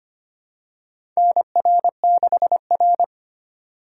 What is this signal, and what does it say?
Morse code, heard as a keyed tone, about 25 wpm, 700 Hz. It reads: NR6R
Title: USA Call Signs - 4 Characters - 3x Character Spacing - 25wpm